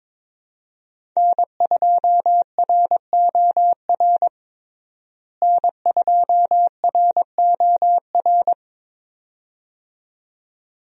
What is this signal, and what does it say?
Morse code, heard as a keyed tone, about 22 wpm, 700 Hz. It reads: N2ROR N2ROR
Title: USA Call Signs - 5 Characters - Repeated 2x - 22wpm